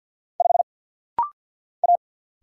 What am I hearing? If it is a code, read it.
Morse code, heard as a keyed tone, about 50 wpm, 700 Hz. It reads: 5 A
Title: Single Letter-Number - Rapid-Fire 50wpm